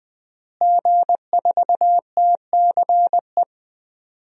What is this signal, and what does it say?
Morse code, heard as a keyed tone, about 20 wpm, 700 Hz. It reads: G4TCE
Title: DXCC-100 Call Signs - 5 Characters - Repeated 3x - 20wpm